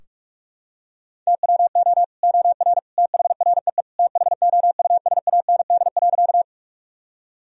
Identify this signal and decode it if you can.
Morse code, heard as a keyed tone, about 46 wpm, 700 Hz. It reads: TWO OR THREE THOUSAND.
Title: Sentences from Top 1000 Words - Review Part 3 - 45wpm